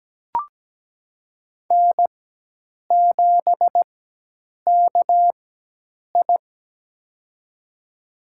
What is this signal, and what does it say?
Morse code, heard as a keyed tone, about 17 wpm, spaced as for 8 wpm, 700 Hz. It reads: N7KI
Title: USA Call Signs - 4 Characters - 4x Character Spacing - 17wpm